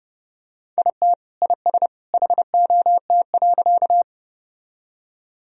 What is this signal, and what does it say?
Morse code, heard as a keyed tone, about 30 wpm, 700 Hz. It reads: IT IS HOT.
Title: Sentences from Top 1000 Words - Review Part 3 - 30wpm